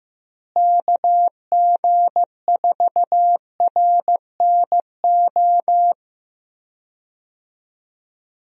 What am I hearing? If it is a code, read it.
Morse code, heard as a keyed tone, about 14 wpm, 700 Hz. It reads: KG4RNO
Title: USA Call Signs - 6 Characters - Repeated 3x - 15wpm